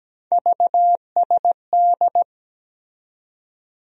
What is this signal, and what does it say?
Morse code, heard as a keyed tone, about 17 wpm, 700 Hz. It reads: VSD
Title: USA Call Sign Postfixes - 3 Letters 17wpm